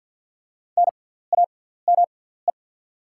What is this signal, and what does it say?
Morse code, heard as a keyed tone, about 48 wpm, spaced as for 17 wpm, 690 Hz. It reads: NAME
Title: 6x Character Spacing - Top 200 Words - 50wpm